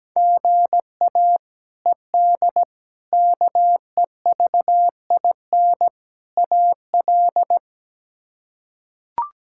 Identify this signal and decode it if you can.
Morse code, heard as a keyed tone, about 17 wpm, 700 Hz. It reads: GA ED KEVIN AL
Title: K1USN SST QSO Practice - 17wpm